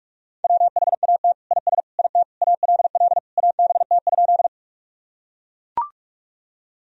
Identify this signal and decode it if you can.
Morse code, heard as a keyed tone, about 42 wpm, 700 Hz. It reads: WHAT IS IT ALL ABT?
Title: Sentences from Top 100 Words and Abbreviations 45wpm